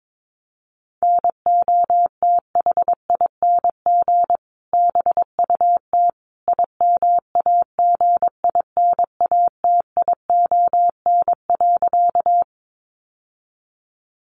Morse code, heard as a keyed tone, about 22 wpm, 700 Hz. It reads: NOTHING BUT IMAGINATION.
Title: Sentences from Top 1300 Words 22wpm